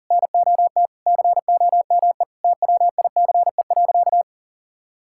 Morse code, heard as a keyed tone, about 40 wpm, 700 Hz. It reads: NOT COME TWICE.
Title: Sentences from Top 1600 Words 40wpm